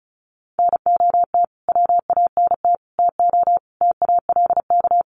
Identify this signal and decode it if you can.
Morse code, heard as a keyed tone, about 34 wpm, 700 Hz. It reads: NOT WANT TO TALK
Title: Sets of 5 Words 35wpm